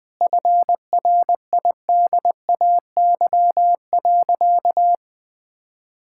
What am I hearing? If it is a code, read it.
Morse code, heard as a keyed tone, about 21 wpm, 700 Hz. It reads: FRIDAY.
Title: Sentences with 7-Word Phrases - 20wpm